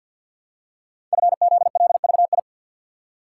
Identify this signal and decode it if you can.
Morse code, heard as a keyed tone, about 52 wpm, 700 Hz. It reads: FZLVI
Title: Code Groups - 5 Letters - 50wpm